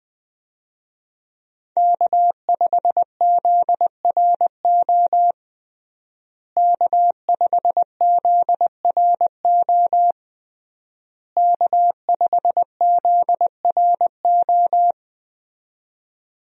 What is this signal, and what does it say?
Morse code, heard as a keyed tone, about 20 wpm, 700 Hz. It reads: K5ZRO K5ZRO K5ZRO
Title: USA Call Signs - 5 Characters - Repeated 3x - 20wpm